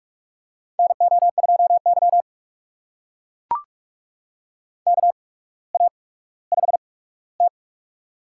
Morse code, heard as a keyed tone, about 44 wpm, spaced as for 19 wpm, 690 Hz. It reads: NO2Y KA5T
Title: USA Call Signs - 4 Characters - 8x Character Spacing - 45wpm